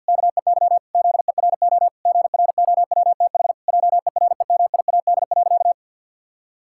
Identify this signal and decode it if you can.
Morse code, heard as a keyed tone, about 50 wpm, 700 Hz. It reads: KEY ZERO GROWTH JEREMIAD.
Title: Pangram Challenge v2 - 50wpm